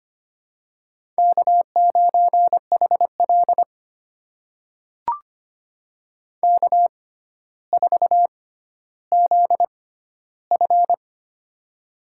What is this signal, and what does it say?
Morse code, heard as a keyed tone, about 25 wpm, spaced as for 13 wpm, 700 Hz. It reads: K9HL K4ZF
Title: USA Call Signs - 4 Characters - 6x Character Spacing - 25wpm